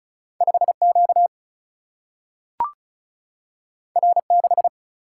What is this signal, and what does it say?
Morse code, heard as a keyed tone, about 36 wpm, 700 Hz. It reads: 5Q R6
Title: Single-Letter Single-Number 35wpm